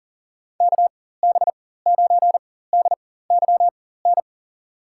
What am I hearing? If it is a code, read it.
Morse code, heard as a keyed tone, about 40 wpm, spaced as for 19 wpm, 700 Hz. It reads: KB9DYN
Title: USA Call Signs - 6 Characters - 4x Character Spacing - 40wpm